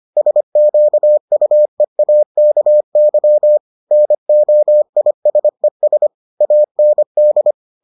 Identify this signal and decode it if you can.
Morse code, heard as a keyed tone, about 25 wpm, 600 Hz. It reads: SQUEAKY NOISES AND